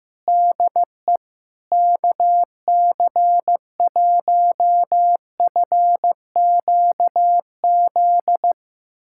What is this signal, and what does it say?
Morse code, heard as a keyed tone, about 15 wpm, 700 Hz. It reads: DE KC1FQZ
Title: USA Callsign DE Callsign 15wpm